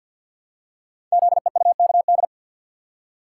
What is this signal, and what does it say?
Morse code, heard as a keyed tone, about 52 wpm, 700 Hz. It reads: ZEUKD
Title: Code Groups - 5 Letters - 50wpm